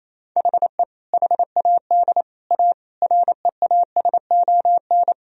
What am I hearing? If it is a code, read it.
Morse code, heard as a keyed tone, about 28 wpm, 700 Hz. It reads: HE HAD A REASON
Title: Sentences from Top 1000 Words - Review Part 1 - 28wpm